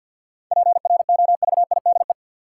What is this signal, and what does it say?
Morse code, heard as a keyed tone, about 53 wpm, 700 Hz. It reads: PROVIDE
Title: Sentences from Top 1100 Words - New Word Drill Alt - 50wpm